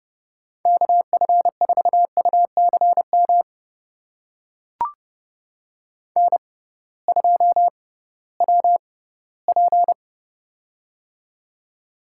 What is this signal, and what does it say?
Morse code, heard as a keyed tone, about 30 wpm, spaced as for 17 wpm, 700 Hz. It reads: KF4UCM N2WP
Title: USA Call Signs - 6x Character Spacing - 30wpm